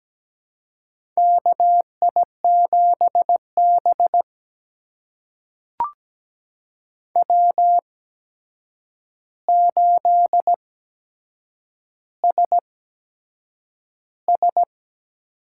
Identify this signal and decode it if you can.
Morse code, heard as a keyed tone, about 17 wpm, spaced as for 7 wpm, 700 Hz. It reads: KI7B W8SS
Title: USA Call Signs - 4 Characters - 8x Character Spacing - 17wpm